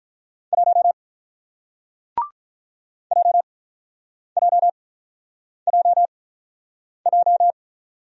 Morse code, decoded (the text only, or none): 1 JJJJ